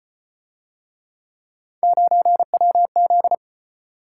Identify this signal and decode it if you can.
Morse code, heard as a keyed tone, about 34 wpm, 700 Hz. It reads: 9WZ